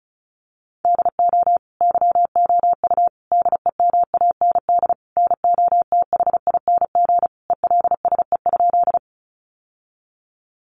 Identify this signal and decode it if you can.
Morse code, heard as a keyed tone, about 34 wpm, 700 Hz. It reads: DO YOU DEMAND NOTHING ELSE?
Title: Sentences from Top 800 Words 35wpm